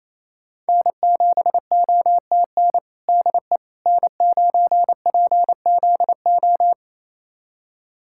Morse code, heard as a keyed tone, about 28 wpm, 700 Hz. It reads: N7OTN DE N9PZO
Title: USA Callsign DE Callsign - 5 Characters - 28wpm